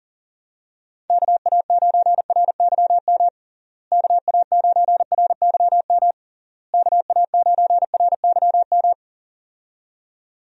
Morse code, heard as a keyed tone, about 40 wpm, 700 Hz. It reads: KA9RYM KA9RYM KA9RYM
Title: USA Call Signs - 6 Characters - Repeated 3x - 40wpm